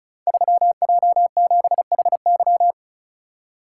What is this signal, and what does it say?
Morse code, heard as a keyed tone, about 36 wpm, 700 Hz. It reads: 3J7HY
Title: Code Groups - 5 Letter:Numbers - 35wpm